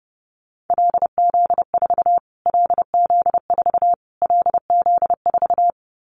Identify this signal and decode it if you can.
Morse code, heard as a keyed tone, about 30 wpm, 700 Hz. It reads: LZ4 LZ4 LZ4
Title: Callsign Prefixes for Top 25 DXCC - Repeated 3x - 30wpm